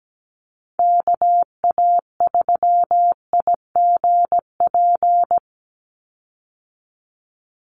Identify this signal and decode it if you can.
Morse code, heard as a keyed tone, about 17 wpm, 700 Hz. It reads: KA3IGP
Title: USA Call Signs 17wpm